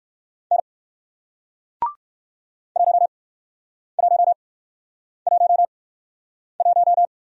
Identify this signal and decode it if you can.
Morse code, heard as a keyed tone, about 56 wpm, spaced as for 9 wpm, 700 Hz. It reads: N 1111
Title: Speed-Racing Single Letter-Number 45wpm